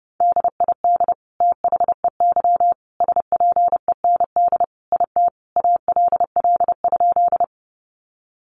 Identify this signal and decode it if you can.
Morse code, heard as a keyed tone, about 30 wpm, 700 Hz. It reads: DID THEY SPEND IT ALL?
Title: Sentences from Top 800 Words 30wpm